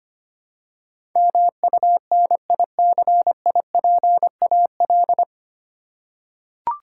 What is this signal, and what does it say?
Morse code, heard as a keyed tone, about 25 wpm, 700 Hz. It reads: MUNICIPAL
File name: practice / 25wpm / 9 Letter Words - First 3 Letters Given - 25wpm